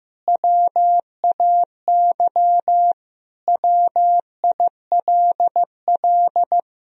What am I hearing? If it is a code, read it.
Morse code, heard as a keyed tone, about 15 wpm, 700 Hz. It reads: WAY WILL
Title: Sentences from Top 1000 Words - Review Part 2 - 15wpm